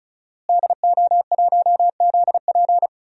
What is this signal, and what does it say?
Morse code, heard as a keyed tone, about 36 wpm, 700 Hz. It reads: DO1ZP
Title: DXCC-100 Call Signs - 5 Characters - Repeated 3x - 35wpm